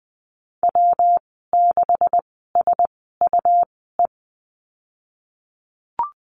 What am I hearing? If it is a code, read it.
Morse code, heard as a keyed tone, about 20 wpm, spaced as for 14 wpm, 700 Hz. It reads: W6SUE
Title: 173 Lesson 28g - Number 9 - Call Signs 20wpm